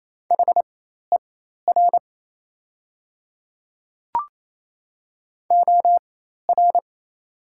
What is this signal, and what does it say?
Morse code, heard as a keyed tone, about 28 wpm, spaced as for 13 wpm, 700 Hz. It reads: HER OR
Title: 4x Character Spacing - Top 300 Words - 28wpm